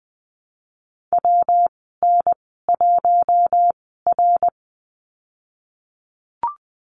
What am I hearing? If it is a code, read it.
Morse code, heard as a keyed tone, about 20 wpm, spaced as for 15 wpm, 700 Hz. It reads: WN1R
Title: 193 Lesson 31g - Letter v - Call Signs 20wpm